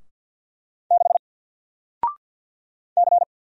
Morse code, heard as a keyed tone, about 50 wpm, 700 Hz. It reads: 6 C
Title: Single Character (Letter:Number:Punctuation:Prosign) Alt 50wpm